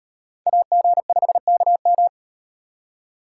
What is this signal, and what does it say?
Morse code, heard as a keyed tone, about 38 wpm, 700 Hz. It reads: AG5KM